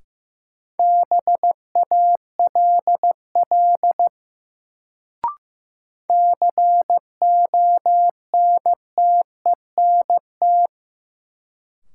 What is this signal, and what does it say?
Morse code, heard as a keyed tone, about 15 wpm, 700 Hz. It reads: BALL CONTENT
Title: Sentences from Top 900 Words - New Word Drill - 15wpm